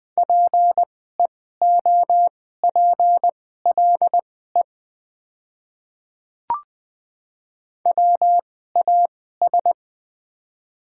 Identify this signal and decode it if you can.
Morse code, heard as a keyed tone, about 20 wpm, spaced as for 15 wpm, 700 Hz. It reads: PEOPLE WAS
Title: 2x Character Spacing - Top 100 Words - 20wpm